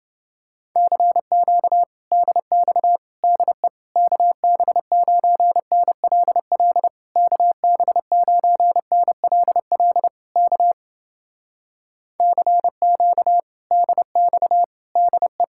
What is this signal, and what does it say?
Morse code, heard as a keyed tone, about 30 wpm, 700 Hz. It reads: CQ DX DE KB9NLL KB9NLL K CQ DX DE